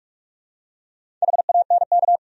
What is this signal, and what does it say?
Morse code, heard as a keyed tone, about 42 wpm, 700 Hz. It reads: HANK
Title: USA Common Names - Max 5 Letters - 45wpm